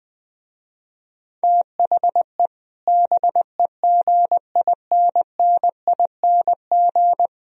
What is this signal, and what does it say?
Morse code, heard as a keyed tone, about 20 wpm, 700 Hz. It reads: THE BEGINNING
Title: Speed-Racing - Sets of 4 Words 17wpm